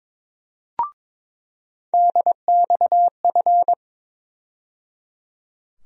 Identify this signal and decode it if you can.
Morse code, heard as a keyed tone, about 22 wpm, 700 Hz. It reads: DXF
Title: USA Call Sign Postfixes - 3 Letters 22wpm